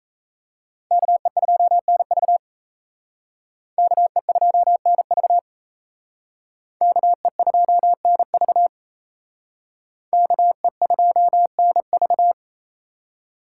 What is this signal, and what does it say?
Morse code, heard as a keyed tone, about 35 wpm, 700 Hz. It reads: KE2NV KE2NV KE2NV KE2NV
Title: Speed Racing - USA Call Signs 28wpm